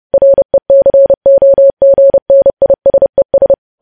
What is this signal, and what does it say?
Morse code, heard as a keyed tone, about 30 wpm, 550 Hz. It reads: RECOGNISES
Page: news headlines in morse code at 30 wpm